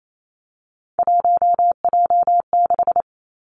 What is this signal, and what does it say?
Morse code, heard as a keyed tone, about 28 wpm, 700 Hz. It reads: 1J6
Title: Three Challenging Letter-Number Combinations 28wpm